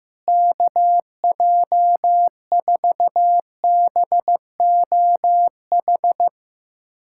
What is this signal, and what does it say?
Morse code, heard as a keyed tone, about 15 wpm, 700 Hz. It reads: KJ4BOH
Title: USA Callsign DE Callsign - 6 Characters - 15wpm